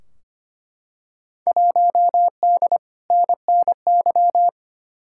Morse code, heard as a keyed tone, about 25 wpm, 700 Hz. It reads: 1D NNY
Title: Field Day Exchanges - 25wpm